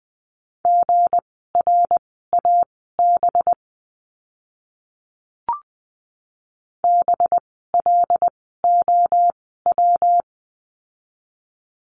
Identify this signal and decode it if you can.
Morse code, heard as a keyed tone, about 20 wpm, spaced as for 14 wpm, 700 Hz. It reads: GRAB BLOW
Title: 185 Lesson 30f - Letter b - Words 20wpm